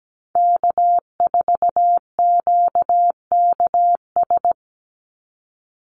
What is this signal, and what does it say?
Morse code, heard as a keyed tone, about 17 wpm, 700 Hz. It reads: K4QKS
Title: USA Call Signs 17wpm